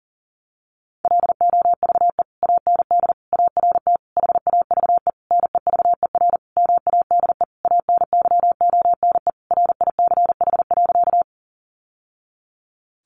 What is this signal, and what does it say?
Morse code, heard as a keyed tone, about 40 wpm, 700 Hz. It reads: LOVE AND ART HAVE NEVER MADE ANYONE RICH.
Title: Sentences from Top 1100 Words 40wpm